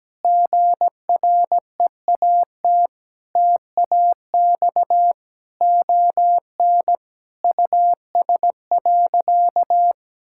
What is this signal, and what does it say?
Morse code, heard as a keyed tone, about 17 wpm, 700 Hz. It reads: GREAT TAX ON US.
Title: Sentences from Top 1000 Words - Review Part 5 - 17wpm